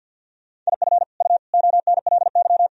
Morse code, heard as a keyed tone, about 51 wpm, 700 Hz. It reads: IF U ONLY
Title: Sentences from Top 400 Words and Abbreviations 50wpm